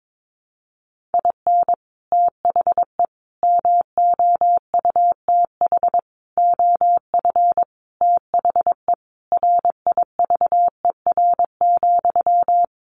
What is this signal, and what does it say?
Morse code, heard as a keyed tone, about 22 wpm, 700 Hz. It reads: IN THE MOUTH OF THE RIVER,
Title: Sentences from Top 1500 Words 22wpm